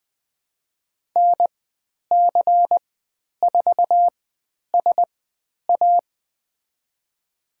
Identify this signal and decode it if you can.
Morse code, heard as a keyed tone, about 20 wpm, spaced as for 10 wpm, 700 Hz. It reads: NC4SA
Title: 095 Lesson 16g - Letter c - Call Signs 20wpm